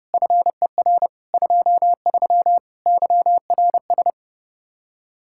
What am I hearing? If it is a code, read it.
Morse code, heard as a keyed tone, about 36 wpm, 700 Hz. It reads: FER 23 YRS
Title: Speed-Racing Common QSO Phrases - 30wpm